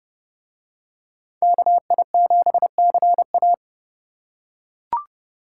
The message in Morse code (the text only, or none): KI7CA